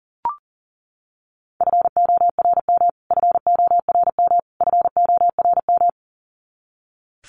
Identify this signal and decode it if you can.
Morse code, heard as a keyed tone, about 40 wpm, 700 Hz. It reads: FORM FORM FORM
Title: Top 200 Words Repeated 40wpm